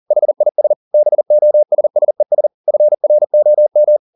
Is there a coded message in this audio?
Yes: Morse code from 0.1 to 4.0 s, heard as a keyed tone, about 40 wpm, 600 Hz. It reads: HIS BOSSES FROM